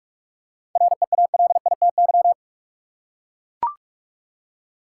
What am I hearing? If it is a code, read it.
Morse code, heard as a keyed tone, about 43 wpm, 700 Hz. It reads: REALITY
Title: Seven Letter Words - 45wpm